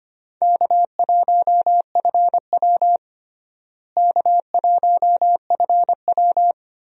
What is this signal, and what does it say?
Morse code, heard as a keyed tone, about 25 wpm, 700 Hz. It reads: K1FW K1FW
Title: USA Call Signs - 4 Characters - Repeated 3x - 25wpm